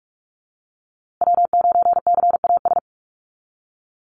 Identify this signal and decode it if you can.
Morse code, heard as a keyed tone, about 44 wpm, 700 Hz. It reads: W9CAS
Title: USA Call Signs Repeated 2x 45wpm